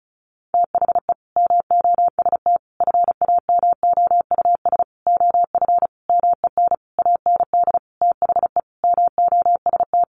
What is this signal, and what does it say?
Morse code, heard as a keyed tone, about 34 wpm, 700 Hz. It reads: THE MOST FAMOUS OF MEN AND THE MOST
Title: Sentences from Top 1500 Words 35wpm